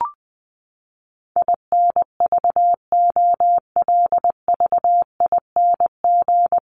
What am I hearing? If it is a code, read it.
Morse code, heard as a keyed tone, about 20 wpm, 700 Hz. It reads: INVOLVING
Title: ING Suffix - Top 100 Words with Base Words 20wpm